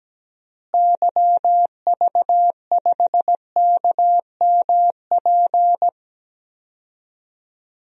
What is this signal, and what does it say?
Morse code, heard as a keyed tone, about 17 wpm, 700 Hz. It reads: YV5KMP
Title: DXCC-100 Call Signs - 6 Characters - Repeated 2x - 17wpm